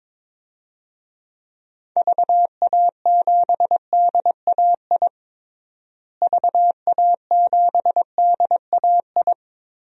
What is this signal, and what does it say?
Morse code, heard as a keyed tone, about 22 wpm, 700 Hz. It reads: VA7DAI VA7DAI
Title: DXCC-100 Call Signs - 6 Characters - Repeated 3x - 22wpm